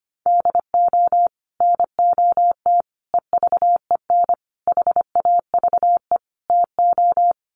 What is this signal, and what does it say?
Morse code, heard as a keyed tone, about 25 wpm, 700 Hz. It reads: DO NOT EVEN HAVE TO
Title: Sets of 5 Words Encore2 25wpm